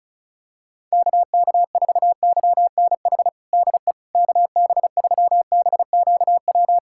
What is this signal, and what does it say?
Morse code, heard as a keyed tone, about 34 wpm, 700 Hz. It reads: KK4YNH DE KB3BQW
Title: USA Callsign DE Callsign - 6 Characters - 35wpm